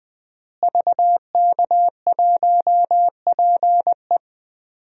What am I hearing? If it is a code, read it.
Morse code, heard as a keyed tone, about 20 wpm, 700 Hz. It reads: VK1PE
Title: DXCC-100 Call Signs - 5 Characters - Repeated 3x - 20wpm